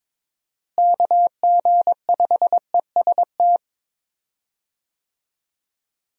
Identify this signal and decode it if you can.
Morse code, heard as a keyed tone, about 22 wpm, 700 Hz. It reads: KG5EST
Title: USA Call Signs - 6 Characters - Repeated 3x - 22wpm